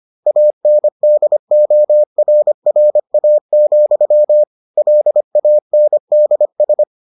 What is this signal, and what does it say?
Morse code, heard as a keyed tone, about 25 wpm, 600 Hz. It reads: ANDORRA, LANDS